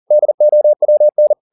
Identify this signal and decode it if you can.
Morse code, heard as a keyed tone, about 40 wpm, 600 Hz. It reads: DOWN